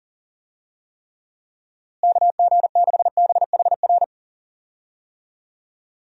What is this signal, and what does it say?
Morse code, heard as a keyed tone, about 40 wpm, 700 Hz. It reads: KG6BHR
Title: USA Call Signs - 6 Characters - Repeated 3x - 40wpm